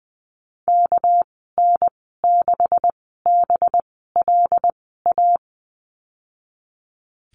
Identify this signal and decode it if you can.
Morse code, heard as a keyed tone, about 20 wpm, spaced as for 15 wpm, 700 Hz. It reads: KN6BLA
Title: 207 Lesson 33g - Letter j - Call Signs 20wpm